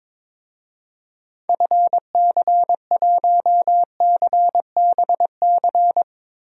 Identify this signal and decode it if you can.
Morse code, heard as a keyed tone, about 22 wpm, 700 Hz. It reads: FC1CBC
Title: DXCC-100 Call Signs - 6 Characters - 22wpm